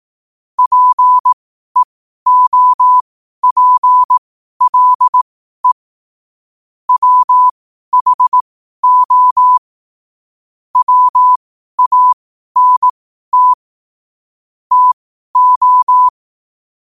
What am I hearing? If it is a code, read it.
Morse code, heard as a keyed tone, about 18 wpm, spaced as for 12 wpm, 1000 Hz. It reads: PEOPLE WHO WANT TO
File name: qotdfiles / 12wpm